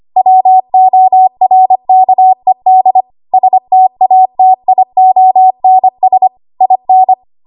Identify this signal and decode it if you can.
Morse code, heard as a keyed tone, about 25 wpm, 750 Hz. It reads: WORKED STATIONS IN
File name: Morse / 25WPM